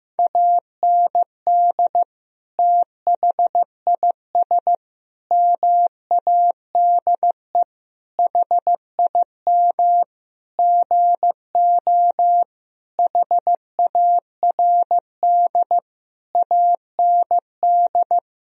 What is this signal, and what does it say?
Morse code, heard as a keyed tone, about 16 wpm, 700 Hz. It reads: AND THIS MADE HIM GO HARD AND
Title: Sentences from Top 1000 Words - Review Part 1 - 15wpm